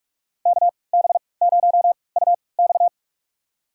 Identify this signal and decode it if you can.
Morse code, heard as a keyed tone, about 46 wpm, spaced as for 25 wpm, 700 Hz. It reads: KB0UX
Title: USA Call Signs - 5 Characters - 3x Character Spacing - 45wpm